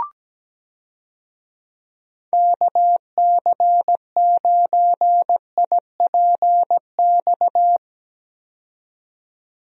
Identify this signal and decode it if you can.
Morse code, heard as a keyed tone, about 17 wpm, 700 Hz. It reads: KC9IPX
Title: USA Call Signs - 6 Characters - 17wpm